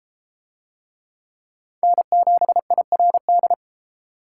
Morse code, heard as a keyed tone, about 31 wpm, 700 Hz. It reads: N7IRD